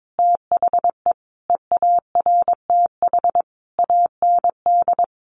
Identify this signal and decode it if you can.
Morse code, heard as a keyed tone, about 22 wpm, 700 Hz. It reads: THE EARTH AND
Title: Sentences from Top 1200 Words 22wpm